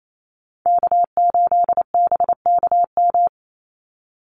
Morse code, heard as a keyed tone, about 28 wpm, 700 Hz. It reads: K8BKM
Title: Call Signs Repeated 2x - 28wpm